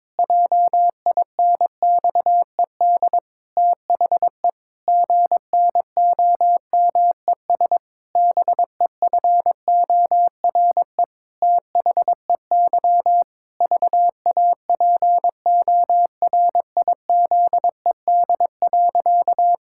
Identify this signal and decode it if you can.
Morse code, heard as a keyed tone, about 22 wpm, 700 Hz. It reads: JINXED THE GNOMES BEFORE THEY VAPORIZED.
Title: Pangram Challenge v2 - 22wpm